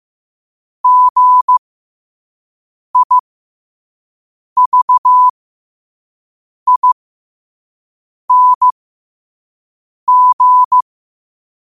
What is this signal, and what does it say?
Morse code, heard as a keyed tone, about 15 wpm, spaced as for 5 wpm, 1000 Hz. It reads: GIVING
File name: qotdfiles / 5wpm